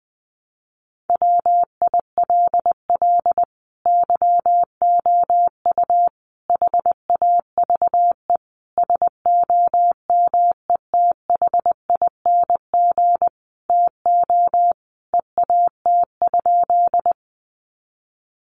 Morse code, heard as a keyed tone, about 20 wpm, 700 Hz. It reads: WILL YOU HAVE SOMETHING TO EAT?
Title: Sentences from Top 600 Words 20wpm